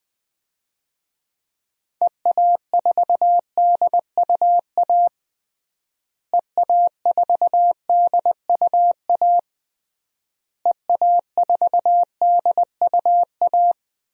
Morse code, heard as a keyed tone, about 21 wpm, 700 Hz. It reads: EA4DUA EA4DUA EA4DUA
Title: DXCC-100 Call Signs - 6 Characters - Repeated 3x - 20wpm